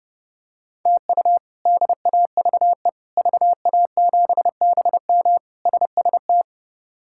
Morse code, heard as a keyed tone, about 30 wpm, 700 Hz. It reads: TU DAVE VA7BM SST
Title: K1USN SST QSO Practice - 30wpm